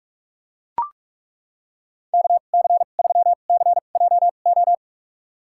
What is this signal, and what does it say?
Morse code, heard as a keyed tone, about 41 wpm, spaced as for 31 wpm, 700 Hz. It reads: KC3CJO
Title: USA Call Signs - 6 Characters - 2x Character Spacing - 45wpm